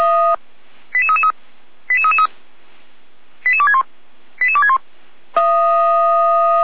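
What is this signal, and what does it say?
Five-tone selective call (ZVEI), heard as quick runs of five tones, tones of about 80 ms. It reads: 88333 88333 88261 88261
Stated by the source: ZVEI five-tone selective call received on an SDR.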